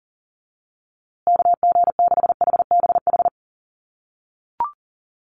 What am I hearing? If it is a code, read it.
Morse code, heard as a keyed tone, about 40 wpm, 700 Hz. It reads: KG6HBH
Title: USA Call Signs Repeated 2x 40wpm